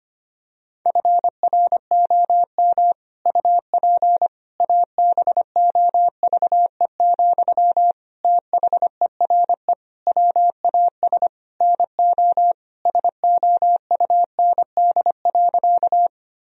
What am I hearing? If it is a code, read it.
Morse code, heard as a keyed tone, about 25 wpm, 700 Hz. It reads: FROM UP ABOVE, THERE WAS NO SOUND.
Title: Sentences from Top 1000 Words - Review Part 2 - 25wpm